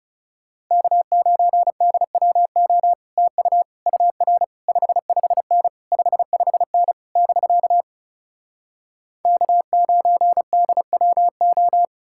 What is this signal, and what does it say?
Morse code, heard as a keyed tone, about 40 wpm, 700 Hz. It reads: K9DWO TU UR 55N 55N <BK> K9DWO